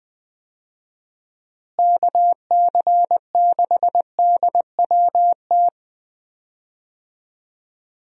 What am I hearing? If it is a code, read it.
Morse code, heard as a keyed tone, about 21 wpm, 700 Hz. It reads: KC6DWT
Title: USA Call Signs - 6 Characters - Repeated 2x - 20wpm